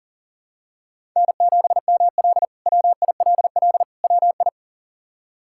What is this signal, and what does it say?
Morse code, heard as a keyed tone, about 40 wpm, 700 Hz. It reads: N7ML WILL WI
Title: K1USN SST QSO Practice - 40wpm